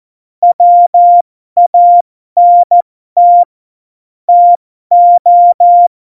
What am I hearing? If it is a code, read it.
Morse code, heard as a keyed tone, about 14 wpm, spaced as for 12 wpm, 700 Hz. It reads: WANT TO
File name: Morsekurs / Witze